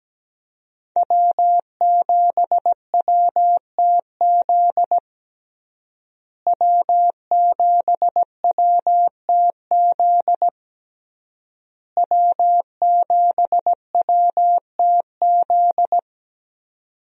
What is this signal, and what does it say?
Morse code, heard as a keyed tone, about 17 wpm, 700 Hz. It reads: W7WTZ W7WTZ W7WTZ
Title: USA Call Signs - 5 Characters - Repeated 3x - 17wpm